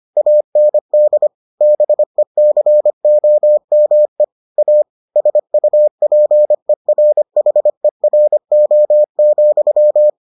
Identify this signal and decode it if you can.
Morse code, heard as a keyed tone, about 25 wpm, 600 Hz. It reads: AND BECOME A SUPERHERO,